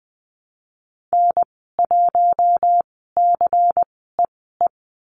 Morse code, heard as a keyed tone, about 20 wpm, spaced as for 15 wpm, 700 Hz. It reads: N1CEE
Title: 186 Lesson 30g - Letter b - Call Signs 20wpm